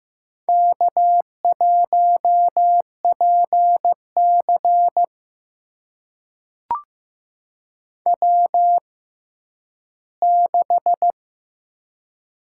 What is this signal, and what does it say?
Morse code, heard as a keyed tone, about 15 wpm, spaced as for 10 wpm, 700 Hz. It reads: K1PC W6
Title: USA Call Signs - 6x Character Spacing - 15wpm